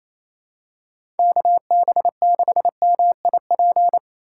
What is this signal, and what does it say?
Morse code, heard as a keyed tone, about 28 wpm, 700 Hz. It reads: KB6MIP
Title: USA Call Signs - 6 Characters - 4x Character Spacing - 28wpm